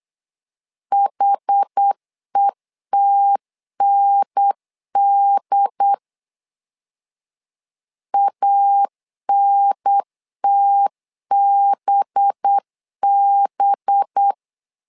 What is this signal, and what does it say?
Morse code, heard as a keyed tone, about 8 wpm, 800 Hz. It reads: HETND ANTBB
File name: LEZIONI / 5 LBVC / 5 LB 8wpm 2816